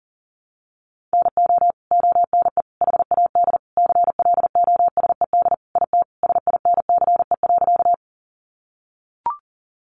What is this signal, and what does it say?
Morse code, heard as a keyed tone, about 40 wpm, 700 Hz. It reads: NO ONE HAD CLOSED IT SINCE.
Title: Sentences from Top 1400 Words 40wpm